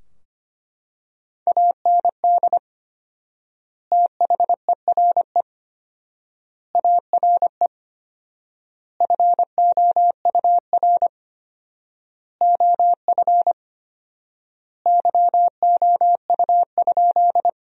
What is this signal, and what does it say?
Morse code, heard as a keyed tone, about 25 wpm, 700 Hz. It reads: AND THERE ARE FOUR OF YOU?
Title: 4x Word-Spacing - Sentences from Top 300 Words - 25wpm